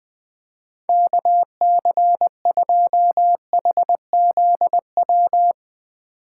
Morse code, heard as a keyed tone, about 20 wpm, 700 Hz. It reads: KC2HZW
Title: USA Call Signs - 6 Characters - 2x Character Spacing - 20wpm